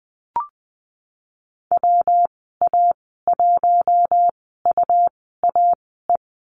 Morse code, heard as a keyed tone, about 19 wpm, spaced as for 14 wpm, 700 Hz. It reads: WA1UAE
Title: 122 Lesson 20g - Number 3 - Call Signs 20wpm